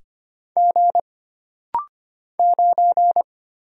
Morse code, heard as a keyed tone, about 25 wpm, 700 Hz. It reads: G 9
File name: Single Character (Letter:Number:Punctuation:Prosign) - Rapid-Fire - 25wpm